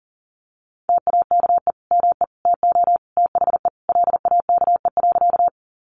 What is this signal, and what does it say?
Morse code, heard as a keyed tone, about 40 wpm, 700 Hz. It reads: TAKE ME TO THE LAKE.
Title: Sentences from Top 1100 Words 40wpm